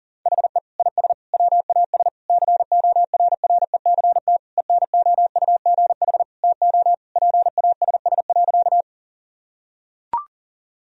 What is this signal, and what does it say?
Morse code, heard as a keyed tone, about 40 wpm, 700 Hz. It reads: HE IS WAS CORRECT ENOUGH TO PASS.
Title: Sentences from Top 1000 Words - Review Part 2 - 40wpm